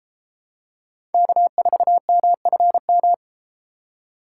Morse code, heard as a keyed tone, about 33 wpm, 700 Hz. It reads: K4MFM